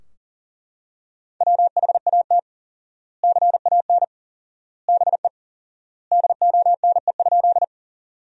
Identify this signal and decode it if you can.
Morse code, heard as a keyed tone, about 40 wpm, 700 Hz. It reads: WHAT CAN BE DONE?
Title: 4x Word-Spacing - Sentences from Top 500 Words - 40wpm